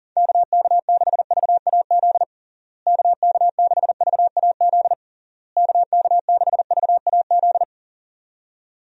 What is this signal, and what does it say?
Morse code, heard as a keyed tone, about 37 wpm, 700 Hz. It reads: KK6VAZ KK6VAZ KK6VAZ
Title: USA Call Signs - 6 Characters - Repeated 3x - 40wpm